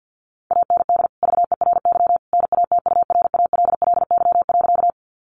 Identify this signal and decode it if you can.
Morse code, heard as a keyed tone, about 50 wpm, 700 Hz. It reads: AND VERY NATURALLY.
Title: Sentences from Top 1300 Words 50wpm